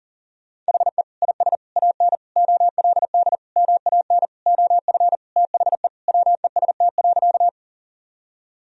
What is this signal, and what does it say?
Morse code, heard as a keyed tone, about 40 wpm, 700 Hz. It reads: HE IS AN OLD MAN OF THE WEST.
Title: Sentences from Top 1000 Words - Review Part 2 - 40wpm